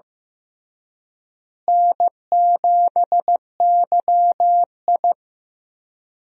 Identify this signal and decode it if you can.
Morse code, heard as a keyed tone, about 16 wpm, 700 Hz. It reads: N7YI
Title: USA Call Signs - 4 Characters - Repeated 2x - 15wpm